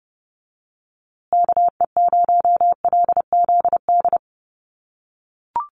Morse code, heard as a keyed tone, about 30 wpm, 700 Hz. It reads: KE0LZD
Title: USA Call Signs Repeated 2x 30wpm